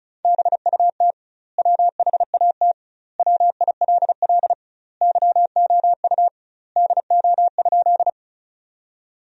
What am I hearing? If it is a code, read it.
Morse code, heard as a keyed tone, about 36 wpm, 700 Hz. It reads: BUT WHAT WILL YOU DO?
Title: 2x Word-Spacing - Sentences from Top 100 Words - 35wpm